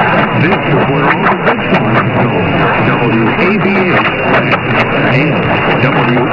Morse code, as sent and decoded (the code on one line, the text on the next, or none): none